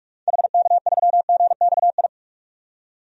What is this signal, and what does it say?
Morse code, heard as a keyed tone, about 45 wpm, 700 Hz. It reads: HK3GXI
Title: DXCC-100 Call Signs - 6 Characters - 45wpm